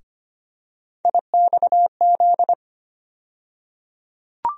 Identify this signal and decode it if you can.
Morse code, heard as a keyed tone, about 25 wpm, 700 Hz. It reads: IXZ
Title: USA Call Sign Postfixes - 3 Letters 25wpm